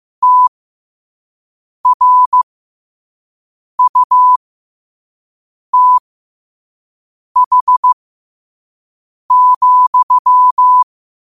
Morse code, heard as a keyed tone, about 15 wpm, spaced as for 5 wpm, 1000 Hz. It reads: TRUTH,
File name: qotdfiles / 5wpm